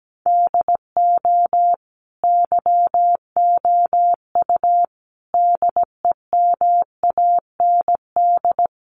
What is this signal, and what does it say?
Morse code, heard as a keyed tone, about 17 wpm, 700 Hz. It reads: DO YOU DEMAND
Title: Sentences from Top 800 Words 17wpm